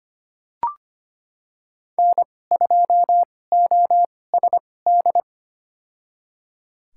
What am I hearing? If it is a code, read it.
Morse code, heard as a keyed tone, about 25 wpm, spaced as for 18 wpm, 700 Hz. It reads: N2OSD
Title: USA Call Signs - 2x Character Spacing - 25wpm